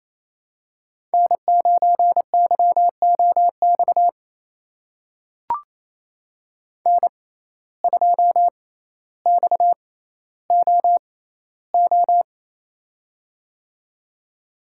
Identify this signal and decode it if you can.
Morse code, heard as a keyed tone, about 28 wpm, spaced as for 14 wpm, 700 Hz. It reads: N9YOX N2XOO
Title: USA Call Signs - 5 Characters - 6x Character Spacing - 28wpm